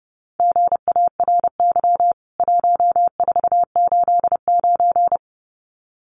Morse code, heard as a keyed tone, about 30 wpm, 700 Hz. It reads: GARY 1489
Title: CWT Exchange 30wpm